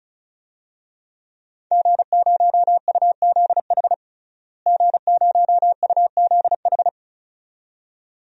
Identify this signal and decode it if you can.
Morse code, heard as a keyed tone, about 36 wpm, 700 Hz. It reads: G0UZH G0UZH
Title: DXCC-100 Call Signs - 5 Characters - Repeated 2x - 35wpm